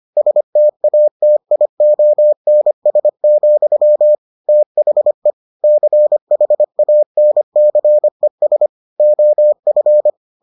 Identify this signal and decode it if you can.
Morse code, heard as a keyed tone, about 25 wpm, 600 Hz. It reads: STATIONS, THE CHANCES OF